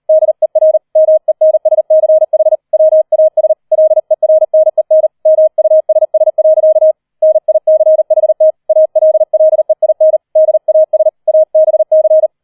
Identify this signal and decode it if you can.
Morse code, heard as a keyed tone, about 38 wpm, 630 Hz. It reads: DER MENSCH WAS LERNEN MUSS. NICHT ALLEIN DAS ABC
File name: 004 Vierter Streich 40wpm